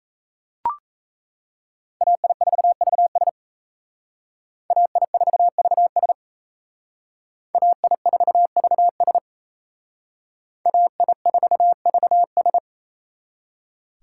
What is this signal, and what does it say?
Morse code, heard as a keyed tone, about 35 wpm, 700 Hz. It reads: AI4VS AI4VS AI4VS AI4VS